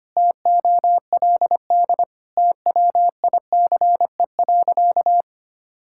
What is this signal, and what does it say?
Morse code, heard as a keyed tone, about 25 wpm, 700 Hz. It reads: TOLD TWICE.
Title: Sentences from Top 1600 Words 25wpm